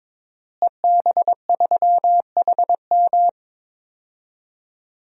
Morse code, heard as a keyed tone, about 22 wpm, 700 Hz. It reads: EB3HM
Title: DXCC-100 Call Signs - 5 Characters - Repeated 2x - 22wpm